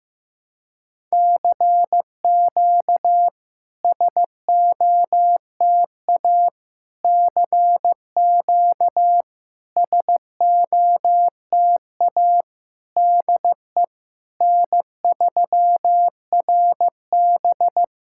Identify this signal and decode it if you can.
Morse code, heard as a keyed tone, about 16 wpm, 700 Hz. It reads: CQ SOTA CQ SOTA DE N3RB
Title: SOTA QSO Practice 15wpm